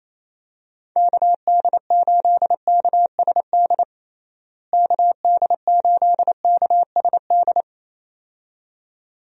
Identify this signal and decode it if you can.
Morse code, heard as a keyed tone, about 28 wpm, 700 Hz. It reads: KD8KSD KD8KSD
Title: USA Call Signs - 6 Characters - Repeated 2x - 28wpm